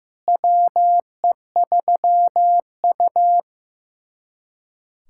Morse code, heard as a keyed tone, about 15 wpm, 700 Hz. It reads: WE3U
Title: USA Callsign DE Callsign - 4 Characters - 15wpm